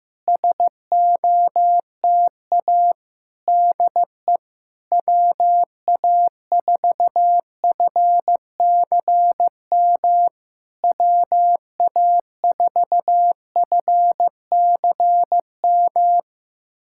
Morse code, heard as a keyed tone, about 15 wpm, 700 Hz. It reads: SOTA DE WA4FCM WA4FCM
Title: SOTA QSO Practice 15wpm